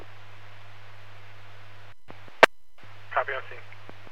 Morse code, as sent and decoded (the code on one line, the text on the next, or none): none